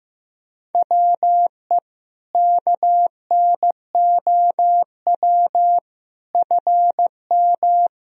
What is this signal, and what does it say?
Morse code, heard as a keyed tone, about 15 wpm, 700 Hz. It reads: WE KNOW FM
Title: Sentences from Top 500 Words and Abbreviations 15wpm